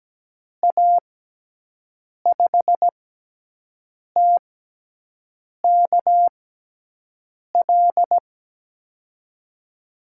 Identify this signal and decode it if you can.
Morse code, heard as a keyed tone, about 17 wpm, spaced as for 6 wpm, 700 Hz. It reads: A5TKL
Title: USA Call Signs - 6 Characters - 6x Character Spacing - 17wpm